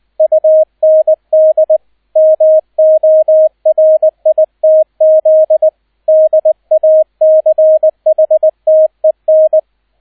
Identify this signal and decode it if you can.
Morse code, heard as a keyed tone, about 20 wpm, 630 Hz, with radio noise. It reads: UND MORITZ DACHTEN